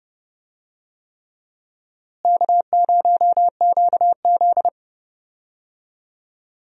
Morse code, heard as a keyed tone, about 30 wpm, 700 Hz. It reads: K0QZ
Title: USA Call Signs - 4 Characters - Repeated 3x - 30wpm